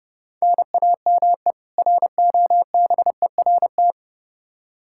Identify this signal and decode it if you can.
Morse code, heard as a keyed tone, about 30 wpm, 700 Hz. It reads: NAME ROBERT
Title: Practice QSO - Exchange 1 - Beginning - 30wpm